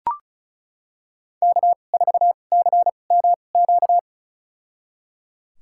Morse code, heard as a keyed tone, about 36 wpm, spaced as for 25 wpm, 700 Hz. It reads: K4CMQ
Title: USA Call Signs - 5 Characters - 2x Character Spacing - 35wpm